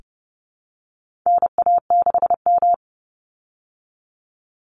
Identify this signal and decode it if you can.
Morse code, heard as a keyed tone, about 30 wpm, 700 Hz. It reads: NA6M
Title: Call Signs Repeated 3x - 30wpm